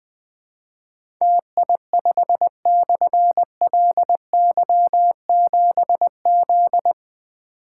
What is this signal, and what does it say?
Morse code, heard as a keyed tone, about 20 wpm, 700 Hz. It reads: TI5/LY7Z
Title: Difficult Call Signs 20wpm